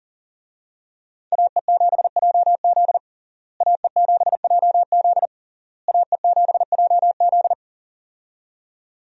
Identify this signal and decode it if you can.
Morse code, heard as a keyed tone, about 40 wpm, 700 Hz. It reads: AE7JZ AE7JZ AE7JZ
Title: USA Call Signs - 5 Characters - Repeated 3x - 40wpm